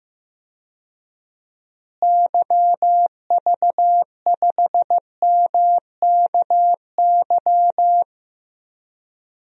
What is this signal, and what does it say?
Morse code, heard as a keyed tone, about 15 wpm, 700 Hz. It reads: YV5MKY
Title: DXCC-100 Call Signs - 6 Characters - Repeated 3x - 15wpm